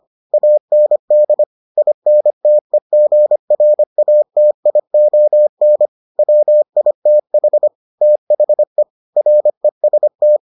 Morse code, heard as a keyed tone, about 25 wpm, 600 Hz. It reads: AND INTEGRATION WITH THE REST